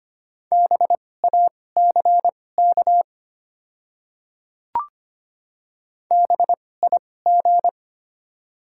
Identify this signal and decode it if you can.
Morse code, heard as a keyed tone, about 25 wpm, spaced as for 18 wpm, 690 Hz. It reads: BACK BIG
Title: 2x Character Spacing - Top 300 Words - 25wpm